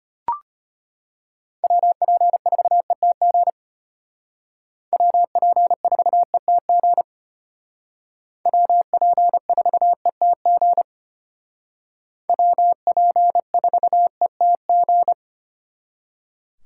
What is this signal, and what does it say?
Morse code, heard as a keyed tone, about 32 wpm, 700 Hz. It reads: WP4ETG WP4ETG WP4ETG WP4ETG